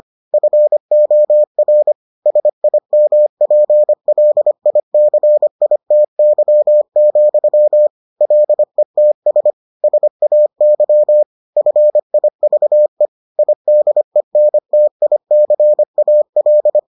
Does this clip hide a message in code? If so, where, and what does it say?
Yes: Morse code from 0.3 to 16.8 s, heard as a keyed tone, about 25 wpm, 600 Hz. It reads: FOR SIMPLICITY, LETS SAY FIVE IDENTICAL